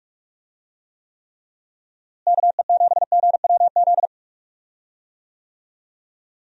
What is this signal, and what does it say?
Morse code, heard as a keyed tone, about 45 wpm, 700 Hz. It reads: KE7GWZ
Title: USA Call Signs - 6 Characters - Repeated 3x - 45wpm